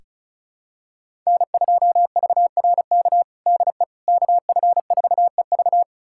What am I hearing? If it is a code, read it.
Morse code, heard as a keyed tone, about 36 wpm, 700 Hz. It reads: N2VRK DE KF4EV
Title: USA Callsign DE Callsign - 5 Characters - 35wpm